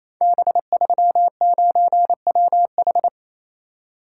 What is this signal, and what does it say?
Morse code, heard as a keyed tone, about 28 wpm, 700 Hz. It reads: B39WH
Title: Code Groups - 5 Letter:Numbers - 28wpm